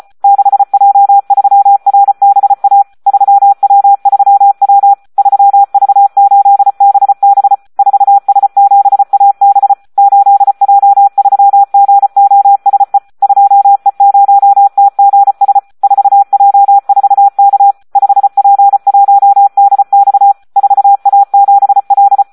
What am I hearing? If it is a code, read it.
Morse code, heard as a keyed tone, about 34 wpm, 800 Hz. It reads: 6J3RBA 3W3W 3V8BB 4S7AB 8J3GOSE 2E0TGS 4J4K 5P1DX 4A7L
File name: Audio practica CW / SD 1 34